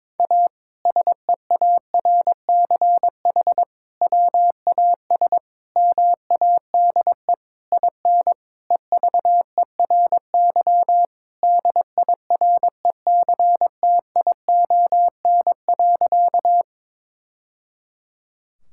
Morse code, heard as a keyed tone, about 22 wpm, 700 Hz. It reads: A SEARCH WAS MADE IN EVERY DIRECTION.
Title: Sentences from Top 1000 Words - Review Part 5 - 22wpm